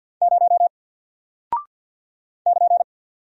Morse code, heard as a keyed tone, about 49 wpm, 700 Hz. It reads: 0 <KN>
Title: Single Character (Letter:Number:Punctuation:Prosign) Alt 50wpm